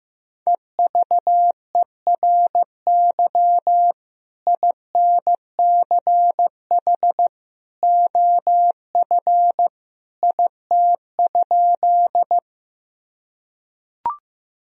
Morse code, heard as a keyed tone, about 15 wpm, 700 Hz. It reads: EVERY INCH OF IT?
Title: Sentences from Top 1000 Words - Review Part 3 - 15wpm